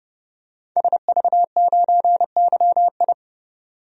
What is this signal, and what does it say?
Morse code, heard as a keyed tone, about 30 wpm, 700 Hz. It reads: SV9YI
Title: DXCC-100 Call Signs - 5 Characters - 30wpm